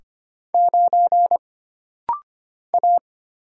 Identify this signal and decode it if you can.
Morse code, heard as a keyed tone, about 25 wpm, 700 Hz. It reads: 9 A
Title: Single Letter-Number - Rapid-Fire 25wpm